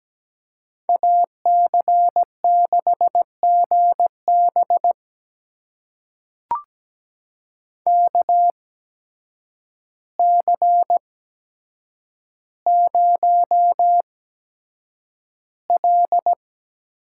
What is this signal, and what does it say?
Morse code, heard as a keyed tone, about 17 wpm, spaced as for 8 wpm, 700 Hz. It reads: AC6GB KC0L
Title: USA Call Signs - 8x Character Spacing - 17wpm